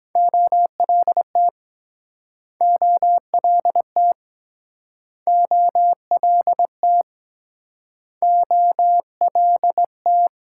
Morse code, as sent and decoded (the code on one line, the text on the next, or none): --- .-.. - / --- .-.. - / --- .-.. - / --- .-.. -
OLT OLT OLT OLT